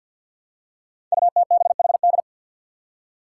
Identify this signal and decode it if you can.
Morse code, heard as a keyed tone, about 52 wpm, 700 Hz. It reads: UTBHD
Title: Five Challenging Letter-Number Combinations 50wpm